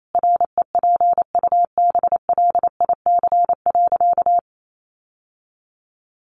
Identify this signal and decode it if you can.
Morse code, heard as a keyed tone, about 28 wpm, 700 Hz. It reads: REPUBLIC.
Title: Sentences from Top 1500 Words 28wpm